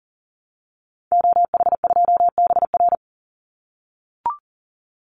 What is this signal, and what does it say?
Morse code, heard as a keyed tone, about 37 wpm, 700 Hz. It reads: OH2BR
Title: Call Signs - 40wpm